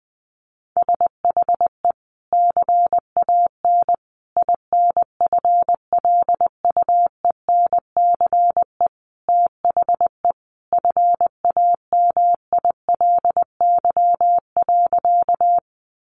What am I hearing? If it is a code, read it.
Morse code, heard as a keyed tone, about 20 wpm, 700 Hz. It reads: SHE CAN INFLUENCE THE FAMILY.
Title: Sentences from Top 1100 Words 20wpm